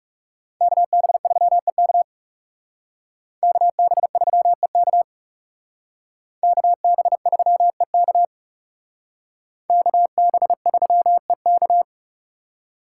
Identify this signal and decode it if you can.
Morse code, heard as a keyed tone, about 36 wpm, 700 Hz. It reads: KB3EK KB3EK KB3EK KB3EK
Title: Speed Racing - USA 2x2 Call Signs 30wpm